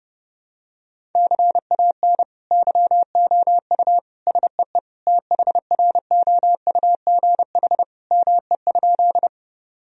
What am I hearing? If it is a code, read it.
Morse code, heard as a keyed tone, about 30 wpm, 700 Hz. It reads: CAN YOU SEE THROUGH ME?
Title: Sentences from Top 1000 Words - Review Part 1 - 30wpm